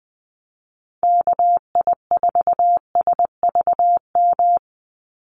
Morse code, heard as a keyed tone, about 20 wpm, 700 Hz. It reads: KI4SVM
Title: NA SOTA Activator Callsigns 20wpm